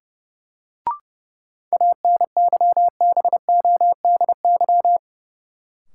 Morse code, heard as a keyed tone, about 30 wpm, 700 Hz. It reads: ANYBODY
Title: Sets of 5 Words - New Word Drill - 30wpm